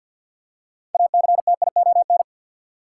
Morse code, heard as a keyed tone, about 46 wpm, 700 Hz. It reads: ACTION
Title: Six Letter Words - 50wpm